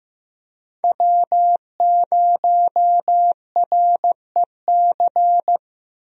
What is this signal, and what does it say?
Morse code, heard as a keyed tone, about 16 wpm, 700 Hz. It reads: W0REC
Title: USA Call Signs - 5 Characters - Repeated 3x - 15wpm